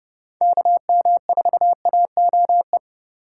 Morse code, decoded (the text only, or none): KM4AOE